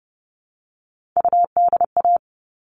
Morse code, heard as a keyed tone, about 30 wpm, 700 Hz. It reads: UDA
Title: All Cut Numbers - 3-Digit 30wpm